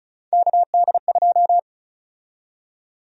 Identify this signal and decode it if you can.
Morse code, heard as a keyed tone, about 34 wpm, 700 Hz. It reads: KD2
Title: USA Call Sign Prefixes - 35wpm